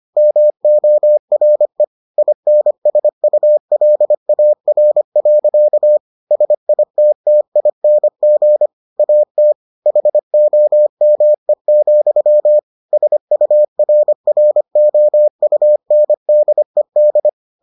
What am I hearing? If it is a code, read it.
Morse code, heard as a keyed tone, about 25 wpm, 600 Hz. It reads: MORE INSULAR. SITTING AT HOME, SURROUNDED